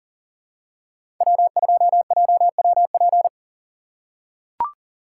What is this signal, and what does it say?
Morse code, heard as a keyed tone, about 40 wpm, 700 Hz. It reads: W2JWP
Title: USA Call Signs - 5 Characters - 2x Character Spacing - 40wpm